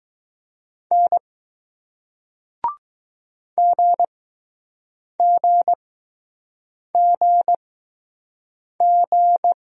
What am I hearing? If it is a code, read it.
Morse code, heard as a keyed tone, about 20 wpm, spaced as for 6 wpm, 700 Hz. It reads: N GGGG